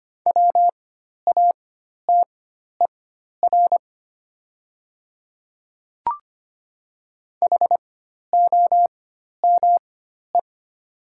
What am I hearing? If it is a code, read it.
Morse code, heard as a keyed tone, about 25 wpm, spaced as for 12 wpm, 700 Hz. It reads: WATER HOME
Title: 4x Character Spacing - Top 300 Words - 25wpm